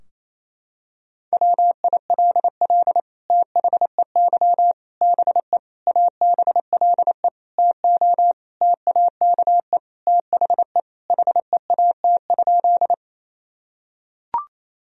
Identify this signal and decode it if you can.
Morse code, heard as a keyed tone, about 28 wpm, 700 Hz. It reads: WILL THEY BE ABLE TO TAKE THE HEAT?
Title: Sentences from Top 1000 Words - Review Part 3 - 28wpm